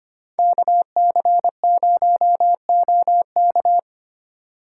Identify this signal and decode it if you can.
Morse code, heard as a keyed tone, about 29 wpm, 700 Hz. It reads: KC0OK
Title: Speed Racing - USA 2x2 Call Signs 25wpm